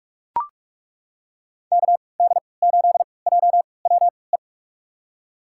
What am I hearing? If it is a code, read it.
Morse code, heard as a keyed tone, about 43 wpm, spaced as for 25 wpm, 700 Hz. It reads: KD8JWE
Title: USA Call Signs - 3x Character Spacing - 45wpm